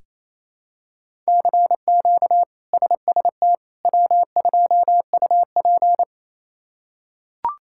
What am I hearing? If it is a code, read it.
Morse code, heard as a keyed tone, about 27 wpm, 700 Hz. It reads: CQ SST W2UP
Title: K1USN SST QSO Practice - 28wpm